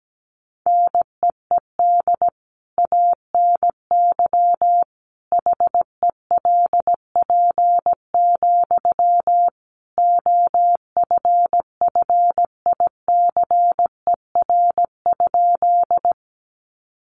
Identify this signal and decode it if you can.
Morse code, heard as a keyed tone, about 17 wpm, 700 Hz. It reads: NEED ANY HELP, OFFICER?
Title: Sentences from Top 1200 Words 17wpm